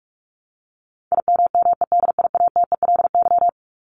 Morse code, heard as a keyed tone, about 44 wpm, 700 Hz. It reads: IMMEDIATELY
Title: LY Suffix - Top 100 Words with Base Words 45wpm